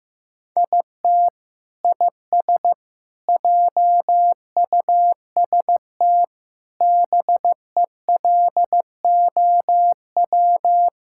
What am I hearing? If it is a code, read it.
Morse code, heard as a keyed tone, about 15 wpm, 700 Hz. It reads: IT IS JUST BELOW
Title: Sentences from Top 1000 Words - Review Part 5 - 15wpm